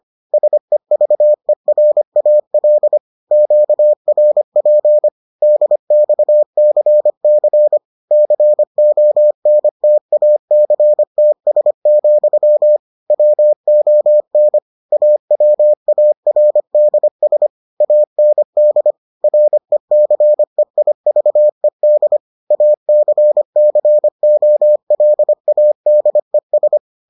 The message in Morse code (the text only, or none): SEVERAL QRP DXCC CONTACTS, WON AWARDS AND RECEIVED ACCOLADES